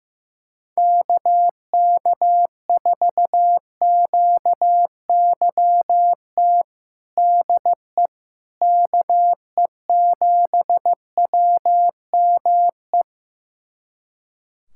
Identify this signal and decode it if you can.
Morse code, heard as a keyed tone, about 15 wpm, 700 Hz. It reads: KK4QYT DE KE7WME
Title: USA Callsign DE Callsign - 6 Characters - 15wpm